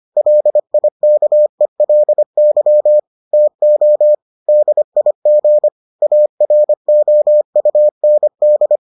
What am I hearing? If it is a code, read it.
Morse code, heard as a keyed tone, about 25 wpm, 600 Hz. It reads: LIKELY TO DIG AROUND